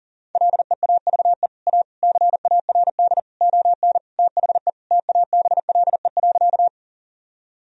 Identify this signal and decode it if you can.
Morse code, heard as a keyed tone, about 40 wpm, 700 Hz. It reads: LEAVE A CARD ON THE TABLE.
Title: Sentences from Top 1000 Words - Review Part 4 - 40wpm